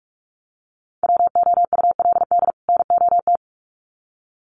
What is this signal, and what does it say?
Morse code, heard as a keyed tone, about 46 wpm, 700 Hz. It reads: WOULD NOT
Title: Sets of 2 Words 45wpm